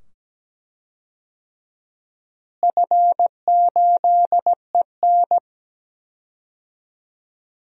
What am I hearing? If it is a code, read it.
Morse code, heard as a keyed tone, about 17 wpm, 700 Hz. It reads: F8EN
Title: DXCC-100 Call Signs - 4 Characters - Repeated 2x - 17wpm